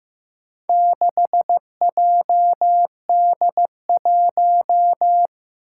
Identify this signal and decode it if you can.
Morse code, heard as a keyed tone, about 15 wpm, 700 Hz. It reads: 6JD1
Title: Four Challenging Letter-Number Combinations 15wpm